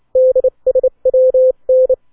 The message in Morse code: DSWN